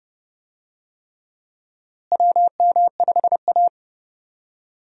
Morse code, heard as a keyed tone, about 30 wpm, 700 Hz. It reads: WM5A
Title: USA Call Signs - 4 Characters - Repeated 3x - 30wpm